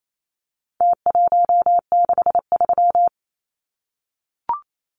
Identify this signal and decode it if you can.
Morse code, heard as a keyed tone, about 28 wpm, 700 Hz. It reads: T163
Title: Numbers - Contest Style - 4-Digit 28wpm